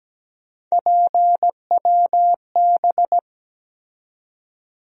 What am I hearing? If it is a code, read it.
Morse code, heard as a keyed tone, about 17 wpm, 700 Hz. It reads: PWB
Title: USA Call Sign Postfixes - 3 Letters 17wpm